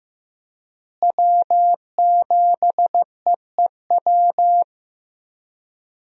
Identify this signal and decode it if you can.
Morse code, heard as a keyed tone, about 15 wpm, 700 Hz. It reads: W7EEW
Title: USA Call Signs - 5 Characters - Repeated 2x - 15wpm